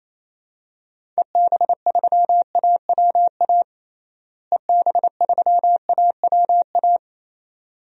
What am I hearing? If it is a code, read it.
Morse code, heard as a keyed tone, about 28 wpm, 700 Hz. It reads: EB3AWA EB3AWA
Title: DXCC-100 Call Signs - 6 Characters - Repeated 2x - 28wpm